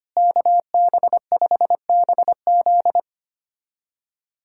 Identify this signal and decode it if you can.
Morse code, heard as a keyed tone, about 25 wpm, 700 Hz. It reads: KB5BZ
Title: USA Callsign DE Callsign - 5 Characters - 25wpm